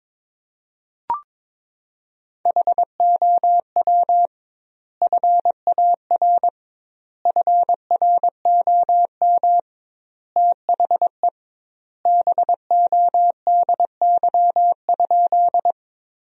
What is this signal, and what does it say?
Morse code, heard as a keyed tone, about 22 wpm, 700 Hz. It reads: HOW FAR FROM THE BODY?
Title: 2x Word-Spacing - Sentences from Top 400 Words - 22wpm